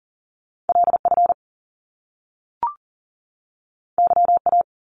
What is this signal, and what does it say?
Morse code, heard as a keyed tone, about 40 wpm, 700 Hz. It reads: LF YA
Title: Top Two Letter Combinations in Words 40wpm